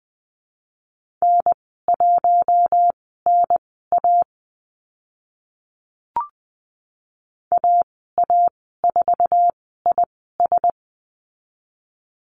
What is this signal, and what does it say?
Morse code, heard as a keyed tone, about 20 wpm, spaced as for 14 wpm, 700 Hz. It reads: N1NA AA4IS
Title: 048 Lesson 9g - Number 4 - Call Signs 20wpm